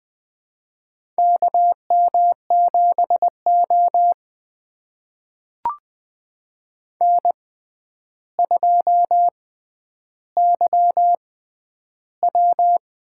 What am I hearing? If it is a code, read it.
Morse code, heard as a keyed tone, about 20 wpm, spaced as for 10 wpm, 700 Hz. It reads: KM7O N2YW
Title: USA Call Signs - 4 Characters - 6x Character Spacing - 20wpm